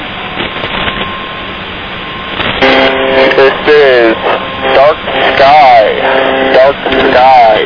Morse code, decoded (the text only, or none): none